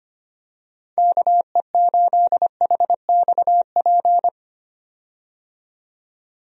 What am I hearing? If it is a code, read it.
Morse code, heard as a keyed tone, about 25 wpm, 700 Hz. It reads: KE8HXP
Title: USA Call Signs - 6 Characters - 25wpm